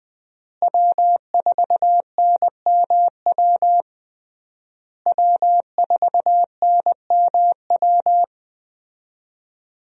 Morse code, heard as a keyed tone, about 21 wpm, 700 Hz. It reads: W4NMW W4NMW
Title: USA Call Signs - 5 Characters - Repeated 3x - 20wpm